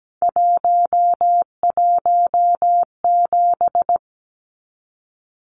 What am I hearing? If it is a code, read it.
Morse code, heard as a keyed tone, about 17 wpm, 700 Hz. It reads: 117
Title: Numbers - 3-Digit 17wpm